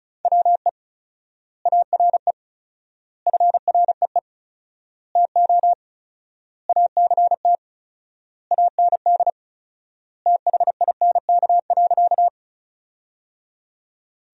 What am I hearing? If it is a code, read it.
Morse code, heard as a keyed tone, about 34 wpm, 700 Hz. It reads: WE ARE FREE TO ACT AND THINK.
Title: 4x Word-Spacing - Sentences from Top 500 Words - 35wpm